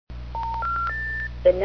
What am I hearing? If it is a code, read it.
Special information tone, with speech.